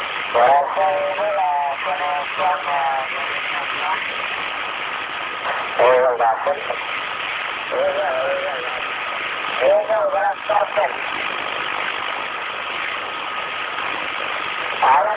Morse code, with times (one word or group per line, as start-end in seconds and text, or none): none